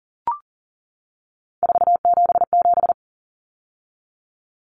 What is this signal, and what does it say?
Morse code, heard as a keyed tone, about 40 wpm, 700 Hz. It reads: 477
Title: Common Cut Numbers - 3-Digit 40wpm